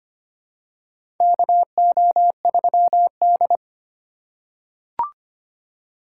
Morse code, heard as a keyed tone, about 25 wpm, 700 Hz. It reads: KO3D
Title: USA Call Signs - 3x Character Spacing - 25wpm